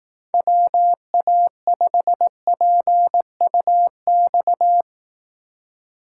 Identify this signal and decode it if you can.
Morse code, heard as a keyed tone, about 18 wpm, 700 Hz. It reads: WA5PUX
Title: Speed Racing - USA 2x3 Call Signs 15wpm